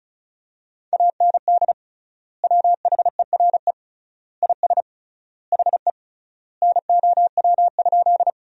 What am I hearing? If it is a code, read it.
Morse code, heard as a keyed tone, about 34 wpm, 700 Hz. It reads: AND WHERE IS HE NOW?
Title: 3x Word-Spacing - Sentences from Top 200 Words - 35wpm